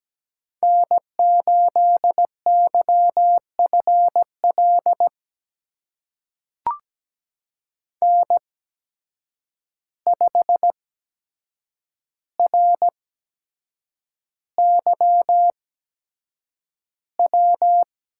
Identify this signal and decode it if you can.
Morse code, heard as a keyed tone, about 17 wpm, spaced as for 7 wpm, 700 Hz. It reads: N8YFL N5RYW
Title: USA Call Signs - 8x Character Spacing - 17wpm